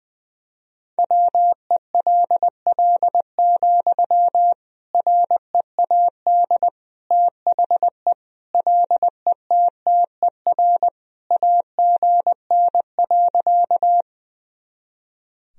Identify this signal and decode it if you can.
Morse code, heard as a keyed tone, about 20 wpm, 700 Hz. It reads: WELL, READ THE LETTER AGN.
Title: Sentences from Top 300 Words and Abbreviations 20wpm